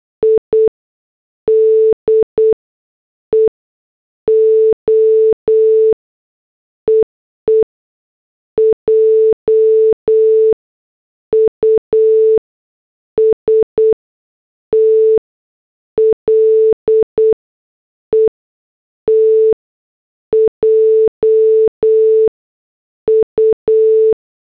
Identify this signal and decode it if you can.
Morse code, heard as a keyed tone, about 8 wpm, 440 Hz. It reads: I D E O EE J U S T L E T J U